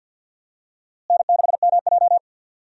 Morse code, heard as a keyed tone, about 53 wpm, 700 Hz. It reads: N6MJ
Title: USA Call Signs - 4 Characters - Repeated 3x - 50wpm